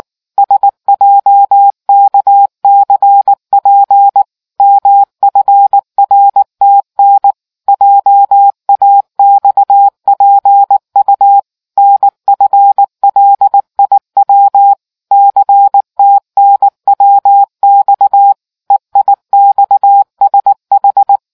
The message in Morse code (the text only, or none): SJKCP MFRTN JAXPU NFLIW CTNWX EIXSH